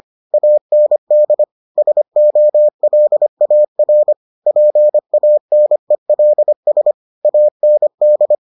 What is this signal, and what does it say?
Morse code, heard as a keyed tone, about 25 wpm, 600 Hz. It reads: AND SOLAR PANELS AND